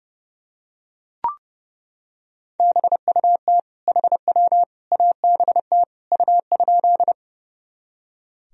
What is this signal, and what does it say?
Morse code, heard as a keyed tone, about 30 wpm, 700 Hz. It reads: BUT HW ABT U?
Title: Sentences from Top 100 Words and Abbreviations 30wpm